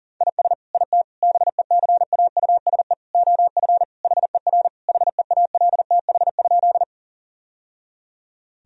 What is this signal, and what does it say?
Morse code, heard as a keyed tone, about 40 wpm, 700 Hz. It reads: IS IT BECAUSE OF HER HEALTH?
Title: Sentences from Top 1000 Words - Review Part 3 - 40wpm